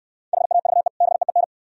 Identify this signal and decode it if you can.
Morse code, heard as a keyed tone, about 64 wpm, 700 Hz. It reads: HAVE BEEN